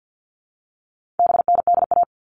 Morse code, heard as a keyed tone, about 50 wpm, 700 Hz. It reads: BNDA
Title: All Cut Numbers - 4-Digit 50wpm